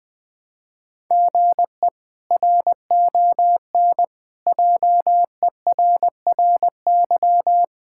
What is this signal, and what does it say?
Morse code, heard as a keyed tone, about 20 wpm, 700 Hz. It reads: GE RON JERRY
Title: K1USN SST QSO Practice - 20wpm